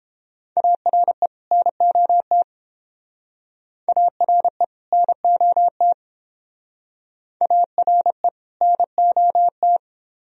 Morse code, heard as a keyed tone, about 31 wpm, 700 Hz. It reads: ARE NOT ARE NOT ARE NOT